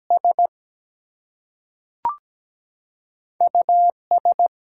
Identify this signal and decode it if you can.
Morse code, heard as a keyed tone, about 17 wpm, 700 Hz. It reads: S US
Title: Two Letter Words - 17wpm